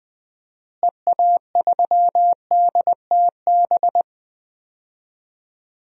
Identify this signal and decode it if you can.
Morse code, heard as a keyed tone, about 20 wpm, 700 Hz. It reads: EA3DTB
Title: DXCC-100 Call Signs - 6 Characters - Repeated 3x - 20wpm